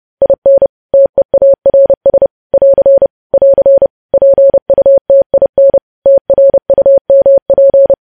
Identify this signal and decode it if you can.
Morse code, heard as a keyed tone, about 30 wpm, 550 Hz. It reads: IN TEARS <AR> <AR> PUTIN TRUMP
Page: news headlines in morse code at 30 wpm